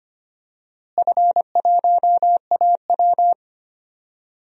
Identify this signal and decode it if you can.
Morse code, heard as a keyed tone, about 25 wpm, 700 Hz. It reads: F1AW
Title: DXCC-100 Call Signs - 4 Characters - Repeated 2x - 25wpm